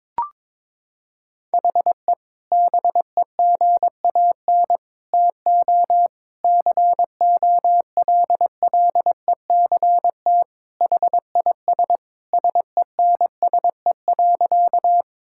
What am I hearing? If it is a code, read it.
Morse code, heard as a keyed tone, about 22 wpm, 700 Hz. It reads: HE BEGAN TO COLLECT HIS SENSE.
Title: Sentences from Top 1600 Words 22wpm